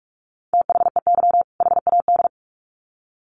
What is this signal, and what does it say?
Morse code, heard as a keyed tone, about 44 wpm, 700 Hz. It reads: THEY HAD
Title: Sets of 2 Words 45wpm